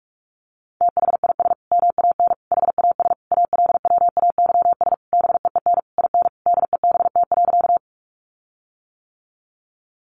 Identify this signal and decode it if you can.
Morse code, heard as a keyed tone, about 44 wpm, 700 Hz. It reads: THIS MAN HAS ALWAYS BEEN IN DEBT.
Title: Sentences from Top 1300 Words 45wpm